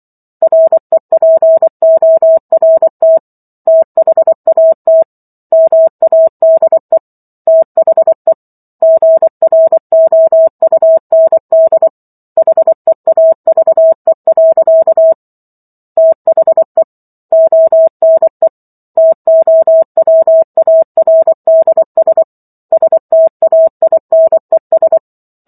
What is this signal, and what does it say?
Morse code, heard as a keyed tone, about 24 wpm, 650 Hz. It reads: REPORT THAT MADE THE GROUND HEAVE. THE ONE TOWARDS STAINES